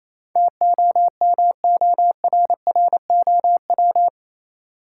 Morse code, heard as a keyed tone, about 28 wpm, 700 Hz. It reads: TOMORROW
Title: Sentences from Top 1100 Words - New Word Drill - 28wpm